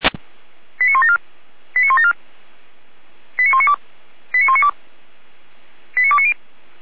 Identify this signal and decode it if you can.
Five-tone selective call (ZVEI), heard as quick runs of five tones, tones of about 70 ms. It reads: 89174 89174 89192 89192 89290